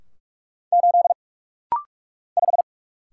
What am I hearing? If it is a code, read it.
Morse code, heard as a keyed tone, about 46 wpm, 700 Hz. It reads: 8 5
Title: Single Letter-Number Alt - Rapid-Fire 45wpm